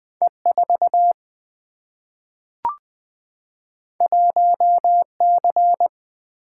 Morse code, heard as a keyed tone, about 20 wpm, 700 Hz. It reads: E4 1C
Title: Single-Letter Single-Number 20wpm